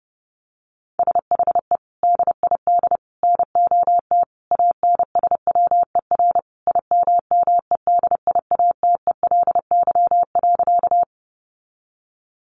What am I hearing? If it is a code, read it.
Morse code, heard as a keyed tone, about 28 wpm, 700 Hz. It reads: SHE DID NOT ANSWER IMMEDIATELY.
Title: Sentences from Top 1100 Words 30wpm